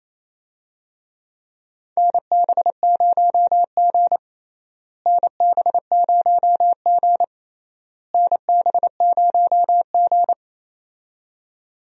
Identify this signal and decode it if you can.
Morse code, heard as a keyed tone, about 28 wpm, 700 Hz. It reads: NB0G NB0G NB0G
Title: USA Call Signs - 4 Characters - Repeated 3x - 28wpm